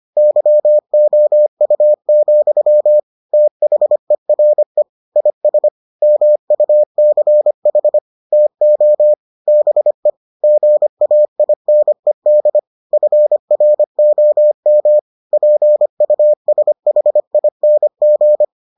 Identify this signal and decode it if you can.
Morse code, heard as a keyed tone, about 25 wpm, 600 Hz. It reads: YOU, THERE IS MUCH TO BE GAINED FROM PUSHING